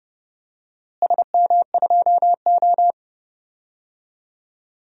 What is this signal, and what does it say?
Morse code, heard as a keyed tone, about 29 wpm, 700 Hz. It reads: SM2O
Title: DXCC-100 Call Signs - 4 Characters - Repeated 3x - 30wpm